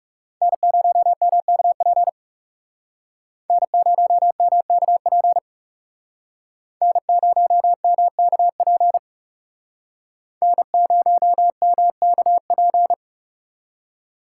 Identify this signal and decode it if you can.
Morse code, heard as a keyed tone, about 39 wpm, 700 Hz. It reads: N0MKP N0MKP N0MKP N0MKP